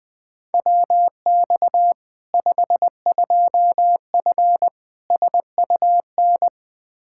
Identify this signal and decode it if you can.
Morse code, heard as a keyed tone, about 20 wpm, 700 Hz. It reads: WX 52F SUN
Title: Practice QSO - Exchange 2 - Info - 20wpm